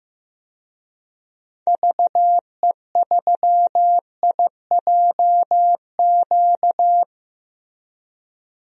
Morse code, heard as a keyed tone, about 15 wpm, 700 Hz. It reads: VE3IJQ
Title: DXCC-100 Call Signs - 6 Characters - Repeated 2x - 15wpm